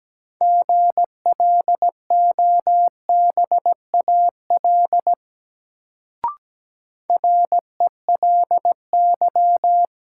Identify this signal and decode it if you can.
Morse code, heard as a keyed tone, about 17 wpm, 700 Hz. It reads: GLOBAL RELY
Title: Sentences from Top 1200 Words - New Word Drill Alt - 17wpm